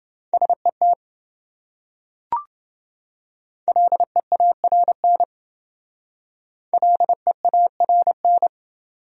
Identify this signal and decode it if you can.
Morse code, heard as a keyed tone, about 28 wpm, 700 Hz. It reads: SET LEARN LEARN